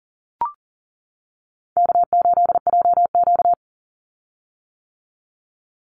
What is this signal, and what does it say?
Morse code, heard as a keyed tone, about 40 wpm, 700 Hz. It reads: K8JQ
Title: Easy Call Signs - 40wpm